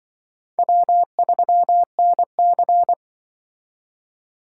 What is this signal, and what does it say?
Morse code, heard as a keyed tone, about 24 wpm, 700 Hz. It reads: W3NC
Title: Speed Racing - USA Call Signs 20wpm